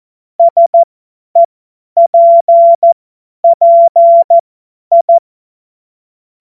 Morse code, heard as a keyed tone, about 14 wpm, spaced as for 10 wpm, 680 Hz. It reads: SEPPI
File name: Morsekurs / Witze / Witz 07. Februar 2023 50 F70